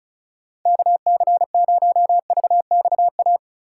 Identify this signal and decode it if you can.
Morse code, heard as a keyed tone, about 37 wpm, 700 Hz. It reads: KC0VXA
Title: Speed Racing - USA 2x3 Call Signs 30wpm